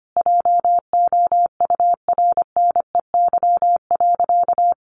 Morse code, heard as a keyed tone, about 25 wpm, 700 Hz. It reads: JOURNEY.
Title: A Sound Of Thunder 25wpm